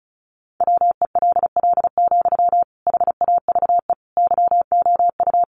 Morse code, heard as a keyed tone, about 34 wpm, 700 Hz. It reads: WELL, HAVE YOU
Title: Sentences from Top 800 Words 35wpm